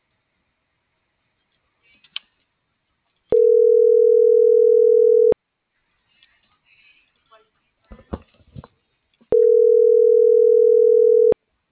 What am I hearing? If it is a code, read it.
Ringback tone (North America), 440 + 480 Hz.